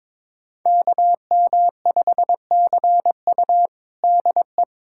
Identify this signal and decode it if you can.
Morse code, heard as a keyed tone, about 22 wpm, 700 Hz. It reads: KM5CU DE
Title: USA Callsign DE Callsign - 5 Characters - 22wpm